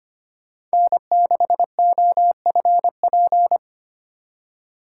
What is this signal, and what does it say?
Morse code, heard as a keyed tone, about 25 wpm, 700 Hz. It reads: N6OFP
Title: USA Call Signs - 5 Characters - 8x Character Spacing - 25wpm